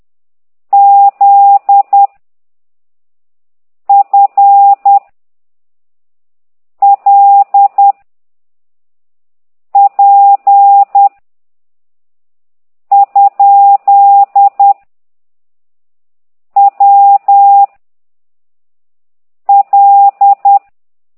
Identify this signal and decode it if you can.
Morse code, heard as a keyed tone, about 10 wpm, 800 Hz. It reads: Z F L P ? W L
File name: ct2idw / exerc15